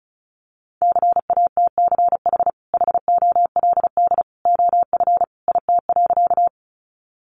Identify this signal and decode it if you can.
Morse code, heard as a keyed tone, about 34 wpm, 700 Hz. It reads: CATCH HOLD OF IT.
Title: Sentences from Top 1500 Words 35wpm